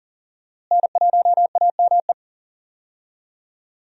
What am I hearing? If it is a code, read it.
Morse code, heard as a keyed tone, about 40 wpm, 700 Hz. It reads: N1AME
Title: USA Call Signs - 5 Characters - Repeated 2x - 40wpm